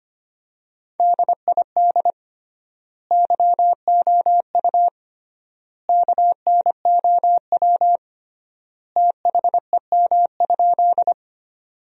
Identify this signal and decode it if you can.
Morse code, heard as a keyed tone, about 25 wpm, 700 Hz. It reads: DID YOU KNOW THEM?
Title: 3x Word-Spacing - Sentences from Top 100 Words - 25wpm